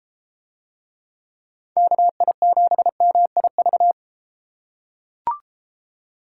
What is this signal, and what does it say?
Morse code, heard as a keyed tone, about 32 wpm, 700 Hz. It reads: KI7MIV